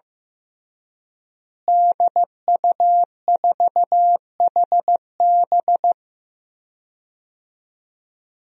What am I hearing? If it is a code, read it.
Morse code, heard as a keyed tone, about 15 wpm, 700 Hz. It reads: DU4HB
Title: DXCC-100 Call Signs - 5 Characters - Repeated 3x - 15wpm